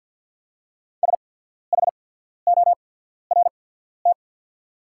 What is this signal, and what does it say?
Morse code, heard as a keyed tone, about 48 wpm, spaced as for 14 wpm, 690 Hz. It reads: SHORT
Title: 8x Character Spacing - Top 500 Words - 50wpm